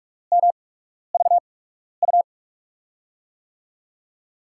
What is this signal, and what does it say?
Morse code, heard as a keyed tone, about 44 wpm, spaced as for 12 wpm, 700 Hz. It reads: MVU
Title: USA Call Signs - 6 Characters - 8x Character Spacing - 45wpm